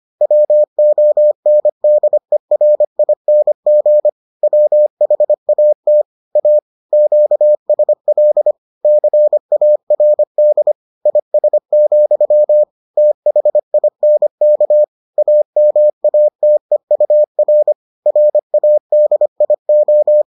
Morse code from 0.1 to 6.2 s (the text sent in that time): WONDERING WHAT